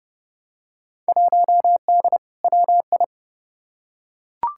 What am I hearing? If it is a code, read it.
Morse code, heard as a keyed tone, about 30 wpm, 700 Hz. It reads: 1D WI
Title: Field Day Exchanges - 30wpm